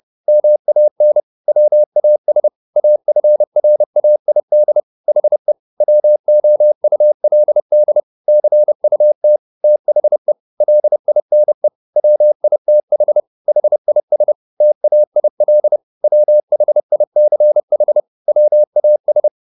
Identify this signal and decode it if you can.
Morse code, heard as a keyed tone, about 30 wpm, 600 Hz. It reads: MAN WAS AFRAID HE WOULD CUT THE LINE WITH HIS TAIL WHICH WAS
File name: the old man and the sea / the old man and the sea 1 30wpm